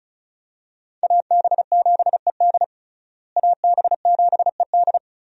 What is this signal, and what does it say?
Morse code, heard as a keyed tone, about 36 wpm, 700 Hz. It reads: AB7ED AB7ED
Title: USA Call Signs - 5 Characters - Repeated 3x - 35wpm